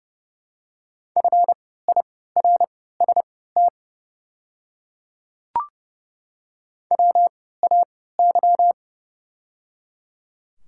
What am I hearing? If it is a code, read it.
Morse code, heard as a keyed tone, about 30 wpm, spaced as for 17 wpm, 700 Hz. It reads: FIRST WAY
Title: 3x Character Spacing - Top 300 Words - 30wpm